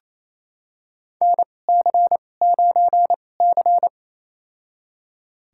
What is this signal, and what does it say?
Morse code, heard as a keyed tone, about 28 wpm, spaced as for 20 wpm, 700 Hz. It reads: NC9C
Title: USA Call Signs - 4 Characters - 2x Character Spacing - 28wpm